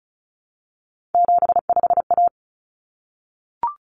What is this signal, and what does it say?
Morse code, heard as a keyed tone, about 34 wpm, 700 Hz. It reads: 75A
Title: Common Cut Numbers - 3-Digit 35wpm